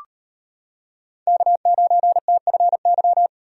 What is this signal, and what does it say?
Morse code, heard as a keyed tone, about 38 wpm, 700 Hz. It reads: K9TFY